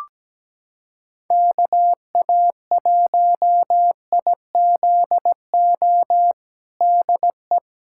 Morse code, heard as a keyed tone, about 17 wpm, 700 Hz. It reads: KA1IZO DE
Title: USA Callsign DE Callsign - 6 Characters - 17wpm